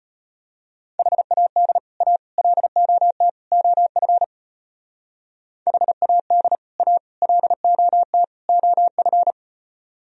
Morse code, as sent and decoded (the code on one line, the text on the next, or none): .... .- -.. / .- / .-.. --- - / --- ..-. / .... .- -.. / .- / .-.. --- - / --- ..-.
HAD A LOT OF HAD A LOT OF